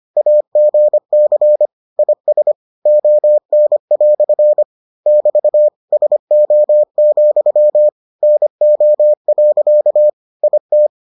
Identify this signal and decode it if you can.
Morse code, heard as a keyed tone, about 25 wpm, 600 Hz. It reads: AGC IS ON" <BT> SO, NO. IT